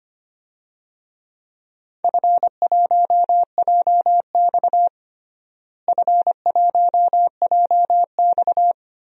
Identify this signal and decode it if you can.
Morse code, heard as a keyed tone, about 25 wpm, 700 Hz. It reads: F1JX F1JX
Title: DXCC-100 Call Signs - 4 Characters - Repeated 2x - 25wpm